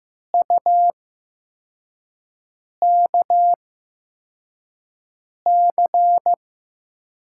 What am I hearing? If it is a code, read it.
Morse code, heard as a keyed tone, about 15 wpm, spaced as for 4 wpm, 700 Hz. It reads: UKC
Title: USA Call Signs - 5 Characters - 8x Character Spacing - 15wpm